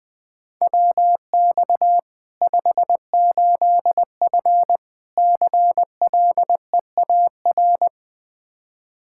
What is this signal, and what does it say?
Morse code, heard as a keyed tone, about 20 wpm, 700 Hz. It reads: WX 58F CLEAR
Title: Practice QSO - Exchange 2 - Info - 20wpm